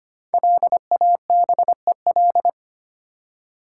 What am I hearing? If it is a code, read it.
Morse code, heard as a keyed tone, about 25 wpm, 700 Hz. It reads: LABEL
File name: 5 Letter Words - First 3 Letters Given - 25wpm